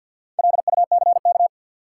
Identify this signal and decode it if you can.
Morse code, heard as a keyed tone, about 47 wpm, 700 Hz. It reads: LUCK
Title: Sentences from Top 1500 Words - New Word Drill - 50wpm